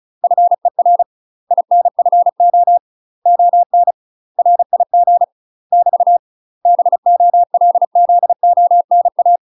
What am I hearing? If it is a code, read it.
Morse code, heard as a keyed tone, about 35 wpm, 700 Hz. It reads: FER INFO ON RIG <BT> BOLZONA
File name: UK MASTER QSOs TRAINING / 35WPM / UK MASTER QSOs 0036